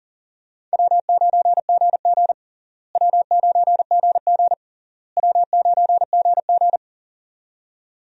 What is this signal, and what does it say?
Morse code, heard as a keyed tone, about 40 wpm, 700 Hz. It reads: W9GG W9GG W9GG
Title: USA Call Signs - 4 Characters - Repeated 3x - 40wpm